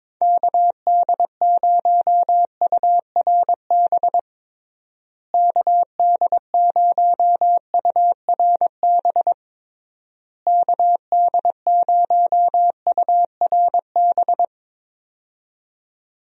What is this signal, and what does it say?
Morse code, heard as a keyed tone, about 22 wpm, 700 Hz. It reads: KD0URB KD0URB KD0URB
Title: USA Call Signs - 6 Characters - Repeated 3x - 22wpm